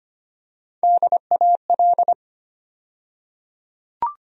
Morse code, heard as a keyed tone, about 25 wpm, 700 Hz. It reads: DAL
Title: USA Call Sign Postfixes - 3 Letters 25wpm